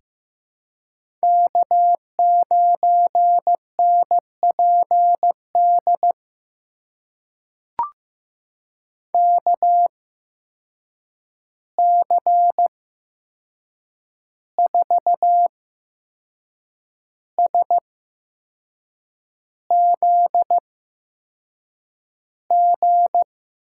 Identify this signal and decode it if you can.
Morse code, heard as a keyed tone, about 15 wpm, spaced as for 6 wpm, 700 Hz. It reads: K9NPD KC4SZG
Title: USA Call Signs - 8x Character Spacing - 15wpm